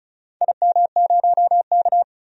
Morse code, heard as a keyed tone, about 34 wpm, 700 Hz. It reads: IM0K
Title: DXCC-100 Call Signs - 4 Characters - 35wpm